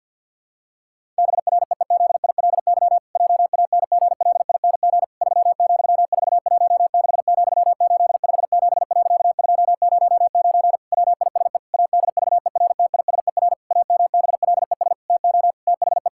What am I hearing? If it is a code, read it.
Morse code, heard as a keyed tone, about 50 wpm, 700 Hz. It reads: BREEZILY JANGLING 3,416,857.209 WISE ADVERTISER AMBLES TO THE
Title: Pangram Challenge v2 - 50wpm